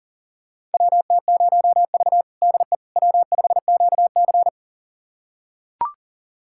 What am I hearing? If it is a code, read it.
Morse code, heard as a keyed tone, about 40 wpm, 700 Hz. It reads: WT0V DE W5QC
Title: USA Callsign DE Callsign - 4 Characters - 40wpm